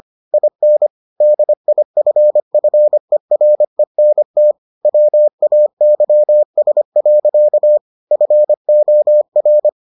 Morse code, heard as a keyed tone, about 25 wpm, 600 Hz. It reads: IN DIFFERENT WAYS. FOR